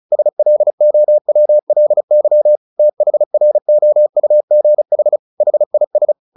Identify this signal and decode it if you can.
Morse code, heard as a keyed tone, about 35 wpm, 600 Hz. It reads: SLOWLY THROUGH HIS
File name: the old man and the sea 2 35wpm